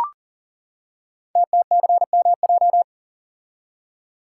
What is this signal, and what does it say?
Morse code, heard as a keyed tone, about 39 wpm, 690 Hz. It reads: TTCMJ
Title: Code Groups - 5 Letters - 40wpm